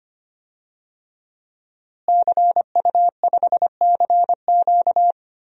Morse code, heard as a keyed tone, about 25 wpm, 700 Hz. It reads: CU5CQ
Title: DXCC-100 Call Signs - 5 Characters - Repeated 3x - 25wpm